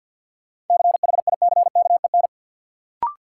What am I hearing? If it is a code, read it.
Morse code, heard as a keyed tone, about 52 wpm, 700 Hz. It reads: CHICKEN
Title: Sentences from Top 1000 Words - Review Part 5 - New Word Drill - 50wpm